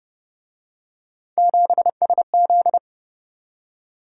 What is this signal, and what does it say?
Morse code, heard as a keyed tone, about 29 wpm, 700 Hz. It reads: 7SZ